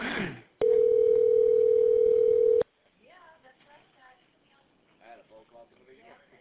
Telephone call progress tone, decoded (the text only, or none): ringback tone